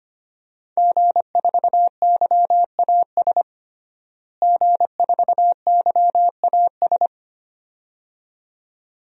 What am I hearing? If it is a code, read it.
Morse code, heard as a keyed tone, about 25 wpm, 700 Hz. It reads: G4YAS G4YAS
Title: DXCC-100 Call Signs - 5 Characters - Repeated 2x - 25wpm